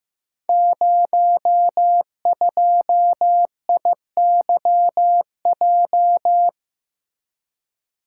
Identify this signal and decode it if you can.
Morse code, heard as a keyed tone, about 15 wpm, 700 Hz. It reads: 02IYJ
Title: Code Groups - 5 Letter:Numbers - 15wpm